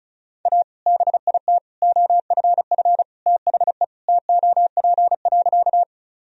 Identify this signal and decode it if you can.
Morse code, heard as a keyed tone, about 36 wpm, 700 Hz. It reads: A BIT OFF THE TOP.
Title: Sentences from Top 1000 Words - Review Part 2 - 35wpm